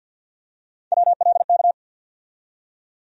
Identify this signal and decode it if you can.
Morse code, heard as a keyed tone, about 53 wpm, 690 Hz. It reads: WLK
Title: USA Call Sign Postfixes - 3 Letters 50wpm